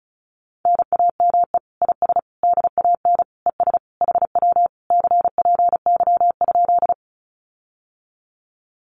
Morse code, heard as a keyed tone, about 36 wpm, 700 Hz. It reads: NAME IS DAN ES HW CPY?
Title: Common QSO Phrases Encore 35wpm